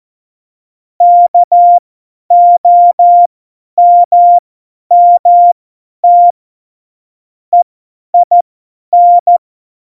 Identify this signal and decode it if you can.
Morse code, heard as a keyed tone, about 14 wpm, spaced as for 10 wpm, 700 Hz. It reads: KOMMT EIN
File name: Morsekurs / Witze / Witz 09. Januar 2018, 50 F70